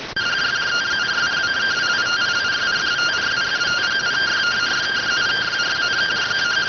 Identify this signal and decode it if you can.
Two-tone teleprinter signal, about 170 Hz shift.